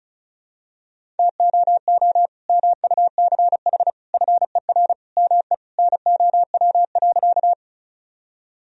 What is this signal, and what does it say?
Morse code, heard as a keyed tone, about 36 wpm, 700 Hz. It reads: TOO MUCH FER ME NOW.
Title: Sentences from Top 200 Words and Abbreviations 35wpm